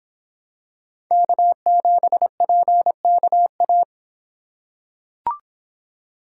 Morse code, heard as a keyed tone, about 26 wpm, 700 Hz. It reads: K7PKA